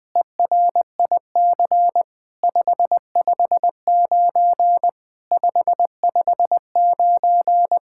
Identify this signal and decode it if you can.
Morse code, heard as a keyed tone, about 20 wpm, 700 Hz. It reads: ERIC 559 559
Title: SOTA QSO Practice 20wpm